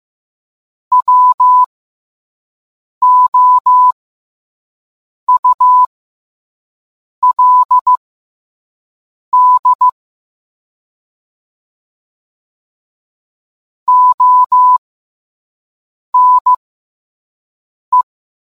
Morse code, heard as a keyed tone, about 15 wpm, spaced as for 5 wpm, 1000 Hz. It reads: WOULD ONE